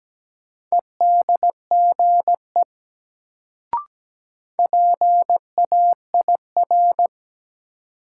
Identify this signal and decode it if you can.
Morse code, heard as a keyed tone, about 17 wpm, 700 Hz. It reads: EDGE PAIR
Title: Sets of 3 Words - New Word Drill - 17wpm